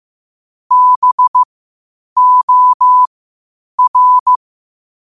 Morse code, heard as a keyed tone, about 16 wpm, spaced as for 8 wpm, 1000 Hz. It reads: BOR
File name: qotdfiles / 8wpm